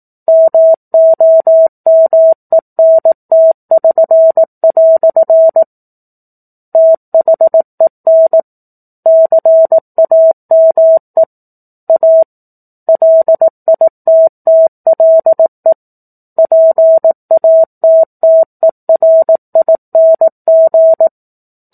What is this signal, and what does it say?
Morse code, heard as a keyed tone, about 18 wpm, 650 Hz. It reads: MOMENT<SN>" THEN CAME A LITTLE PATTERING
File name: 04-Chapter-04 18 WPM